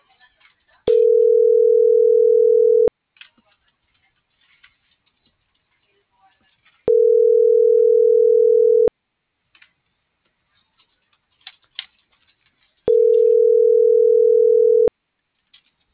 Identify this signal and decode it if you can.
Ringback tone (North America), 440 + 480 Hz.